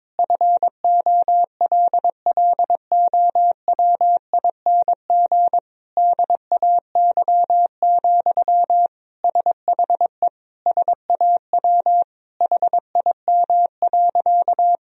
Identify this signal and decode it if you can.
Morse code, heard as a keyed tone, about 22 wpm, 700 Hz. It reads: FOLLOWING DAY, SHE SAW HIM.
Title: Sentences from Top 1000 Words - Review Part 4 - 22wpm